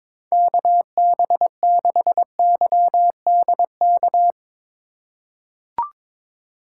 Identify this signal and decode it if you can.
Morse code, heard as a keyed tone, about 21 wpm, 700 Hz. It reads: KB6YDK
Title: USA Call Signs - 6 Characters - 3x Character Spacing - 22wpm